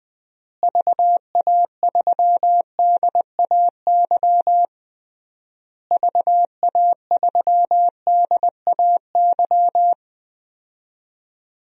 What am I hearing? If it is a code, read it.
Morse code, heard as a keyed tone, about 20 wpm, 700 Hz. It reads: VA3DAY VA3DAY
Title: DXCC-100 Call Signs - 6 Characters - Repeated 3x - 20wpm